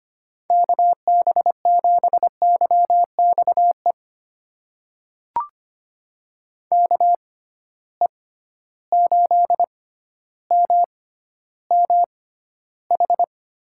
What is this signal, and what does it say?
Morse code, heard as a keyed tone, about 25 wpm, spaced as for 13 wpm, 700 Hz. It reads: KB7YXE KE8MMH
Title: USA Call Signs - 6 Characters - 6x Character Spacing - 25wpm